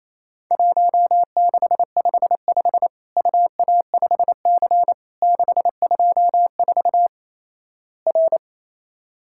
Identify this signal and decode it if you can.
Morse code, heard as a keyed tone, about 28 wpm, 700 Hz. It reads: 1655 UA5C 624 R
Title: Worked All Europe DX Contest - QTC Exchange - 28wpm